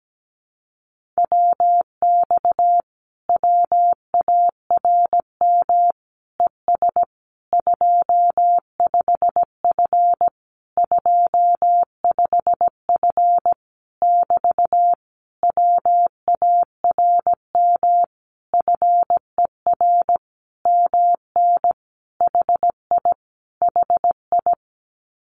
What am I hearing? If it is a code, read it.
Morse code, heard as a keyed tone, about 17 wpm, 700 Hz. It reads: WX WARM ES 25F 25F <BT> WARM FER MN HI HI
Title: Common QSO Phrases Encore 17wpm